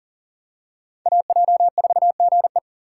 Morse code, heard as a keyed tone, about 40 wpm, 700 Hz. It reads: AJ4GE
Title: USA Call Signs - 5 Characters - 6x Character Spacing - 40wpm